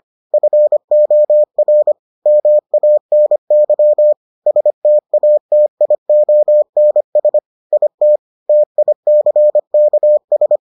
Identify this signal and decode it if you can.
Morse code, heard as a keyed tone, about 25 wpm, 600 Hz. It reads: FOR MANY STATIONS IT TICKS